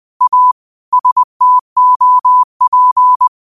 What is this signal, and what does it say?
Morse code, heard as a keyed tone, about 20 wpm, 1000 Hz. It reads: A STOP
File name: qotdfiles / 20wpm